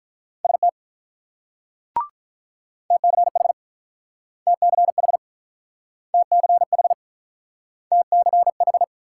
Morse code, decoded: ST TCH TCH TCH TCH

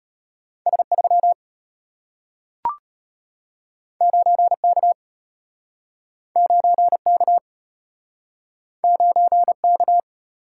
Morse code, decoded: S3 9K 9K 9K

